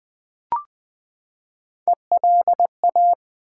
Morse code, heard as a keyed tone, about 20 wpm, 700 Hz. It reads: ELA
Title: Build A Word - 3 Letters At A Time - 9-Letter Words - 20wpm